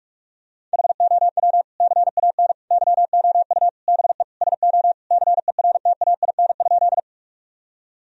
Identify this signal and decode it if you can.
Morse code, heard as a keyed tone, about 46 wpm, 700 Hz. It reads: HOW CAN YOU BE SO CERTAIN?
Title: Sentences from Top 1000 Words - Review Part 2 - 45wpm